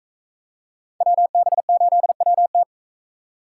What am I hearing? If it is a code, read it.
Morse code, heard as a keyed tone, about 42 wpm, 700 Hz. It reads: WB8WT